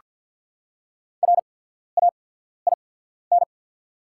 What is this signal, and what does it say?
Morse code, heard as a keyed tone, about 48 wpm, spaced as for 14 wpm, 690 Hz. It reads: RAIN
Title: 8x Character Spacing - Top 500 Words - 50wpm